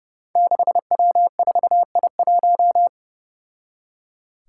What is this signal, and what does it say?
Morse code, heard as a keyed tone, about 30 wpm, 700 Hz. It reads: 6W4I1
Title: Five Challenging Letter-Number Combinations 30wpm